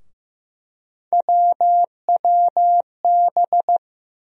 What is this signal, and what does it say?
Morse code, heard as a keyed tone, about 15 wpm, 700 Hz. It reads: WWB
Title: USA Call Sign Postfixes - 3 Letters 15wpm